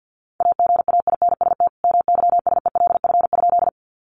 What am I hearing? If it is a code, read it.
Morse code, heard as a keyed tone, about 50 wpm, 700 Hz. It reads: AGAINST MYSELF?
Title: Sentences from Top 1300 Words 50wpm